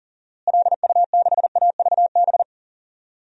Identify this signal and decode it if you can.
Morse code, heard as a keyed tone, about 40 wpm, 700 Hz. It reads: LU6AVB
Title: DXCC-100 Call Signs - 6 Characters - Repeated 3x - 40wpm